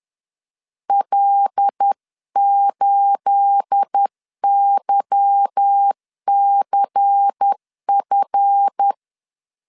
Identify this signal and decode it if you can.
Morse code, heard as a keyed tone, about 10 wpm, 800 Hz. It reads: L8YCF
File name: LEZIONI / 9 28 10wpm 2842